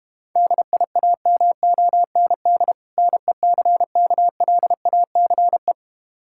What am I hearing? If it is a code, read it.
Morse code, heard as a keyed tone, about 32 wpm, 700 Hz. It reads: DIAMOND NECKLACE